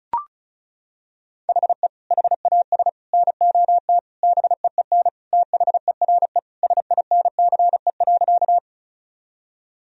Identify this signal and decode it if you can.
Morse code, heard as a keyed tone, about 34 wpm, 700 Hz. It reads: HE HAS NOT BEEN THERE SINCE.
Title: Sentences from Top 1000 Words - Review Part 1 - 35wpm